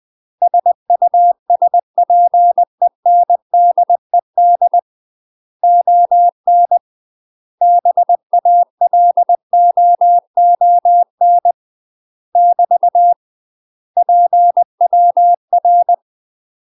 Morse code, heard as a keyed tone, about 20 wpm, 700 Hz. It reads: SUSPENDED ON BALOON <BT> PWR
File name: UK MASTER QSOs TRAINING / 20WPM / UK MASTER QSOs 0031